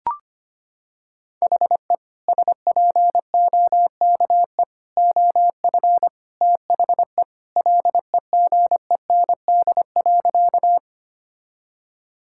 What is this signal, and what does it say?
Morse code, heard as a keyed tone, about 25 wpm, 700 Hz. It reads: HE SPOKE OF THE LEGEND.
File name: practice / 25wpm / Sentences with 2-Word Phrases - 25wpm